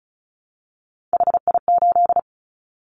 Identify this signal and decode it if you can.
Morse code, heard as a keyed tone, about 34 wpm, 700 Hz. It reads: HI8
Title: Callsign Prefixes for Top 100 DXCC - 35wpm